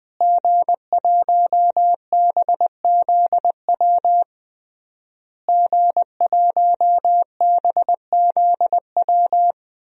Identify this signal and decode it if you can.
Morse code, heard as a keyed tone, about 20 wpm, 700 Hz. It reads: G1BZW G1BZW
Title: DXCC-100 Call Signs - 5 Characters - Repeated 3x - 20wpm